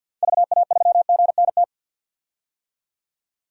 Morse code, heard as a keyed tone, about 50 wpm, 700 Hz. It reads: VA3GNT
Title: DXCC-100 Call Signs - 6 Characters - 50wpm